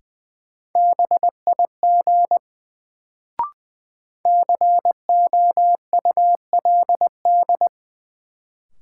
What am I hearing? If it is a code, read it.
Morse code, heard as a keyed tone, about 20 wpm, 700 Hz. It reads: BIG COULD
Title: Sets of 2 Words - New Word Drill - 20wpm